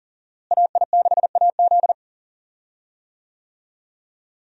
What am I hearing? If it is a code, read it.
Morse code, heard as a keyed tone, about 40 wpm, 700 Hz. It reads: AI6AZ
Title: USA Call Signs - 5 Characters - Repeated 2x - 40wpm